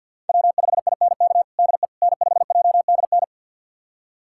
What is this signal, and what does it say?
Morse code, heard as a keyed tone, about 50 wpm, 700 Hz. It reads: W5INK DE N5JDN
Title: USA Callsign DE Callsign - 5 Characters - 50wpm